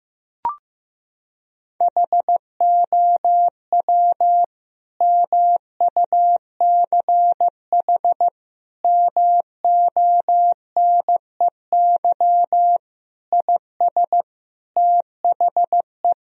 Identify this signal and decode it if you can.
Morse code, heard as a keyed tone, about 15 wpm, 700 Hz. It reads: HOW MUCH MONEY IS THE
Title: Sentences from Top 1000 Words - Review Part 4 - 15wpm